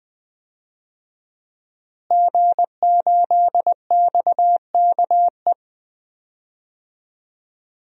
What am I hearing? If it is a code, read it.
Morse code, heard as a keyed tone, about 20 wpm, 700 Hz. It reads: G8XKE
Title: DXCC-100 Call Signs - 5 Characters - Repeated 2x - 20wpm